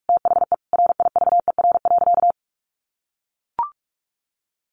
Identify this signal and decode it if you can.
Morse code, heard as a keyed tone, about 44 wpm, 700 Hz. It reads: THE RIVER.
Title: Sentences from Top 1300 Words 45wpm